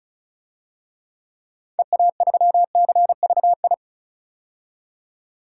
Morse code, heard as a keyed tone, about 36 wpm, 700 Hz. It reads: EA3CVI
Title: DXCC-100 Call Signs - 6 Characters - 35wpm